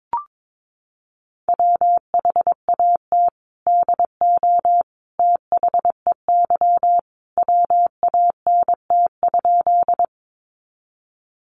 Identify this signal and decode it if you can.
Morse code, heard as a keyed tone, about 22 wpm, 700 Hz. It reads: WHAT DO THEY WANT?
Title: Sentences from Top 200 Words 22wpm